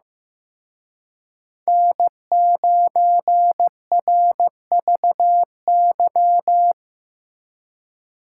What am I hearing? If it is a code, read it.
Morse code, heard as a keyed tone, about 17 wpm, 700 Hz. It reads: N9RVY
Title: Speed Racing - USA 1x3 Call Signs 15wpm